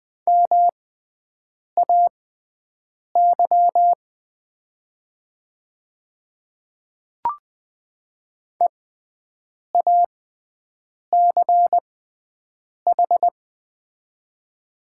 Morse code, heard as a keyed tone, about 20 wpm, spaced as for 7 wpm, 700 Hz. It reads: MAY EACH
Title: 6x Character Spacing - Top 200 Words - 20wpm